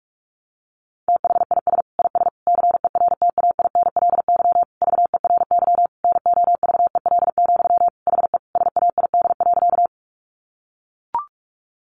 Morse code, heard as a keyed tone, about 44 wpm, 700 Hz. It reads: THIS IS CERTAINLY VERY NOVEL, HE SAID.
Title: Sentences from Top 1300 Words 45wpm